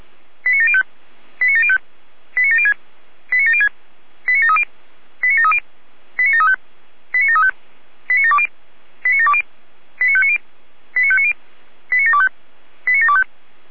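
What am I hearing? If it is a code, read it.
Five-tone selective call (ZVEI), heard as quick runs of five tones, tones of about 70 ms. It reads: 89705 89705 89706 89706 89730 89730 89735 89735 89720 89720 89590 89590 89725 89725
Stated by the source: ZVEI five-tone selective call received with an SDR.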